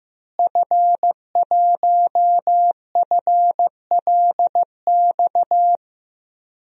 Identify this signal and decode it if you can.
Morse code, heard as a keyed tone, about 15 wpm, 700 Hz. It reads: F1FLX
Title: DXCC-100 Call Signs - 5 Characters - Repeated 3x - 15wpm